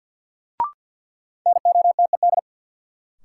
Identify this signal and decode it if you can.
Morse code, heard as a keyed tone, about 49 wpm, 700 Hz. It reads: NOTED
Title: Sets of 5 Words - New Word Drill - 50wpm